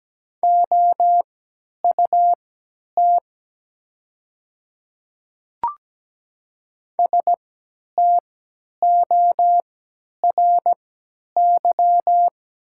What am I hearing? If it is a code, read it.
Morse code, heard as a keyed tone, about 17 wpm, spaced as for 10 wpm, 700 Hz. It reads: OUT STORY
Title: 3x Character Spacing - Top 300 Words - 17wpm